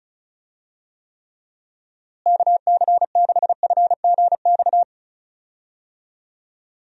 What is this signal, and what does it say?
Morse code, heard as a keyed tone, about 36 wpm, 700 Hz. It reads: KC6FGX
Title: USA Call Signs - 6 Characters - Repeated 3x - 35wpm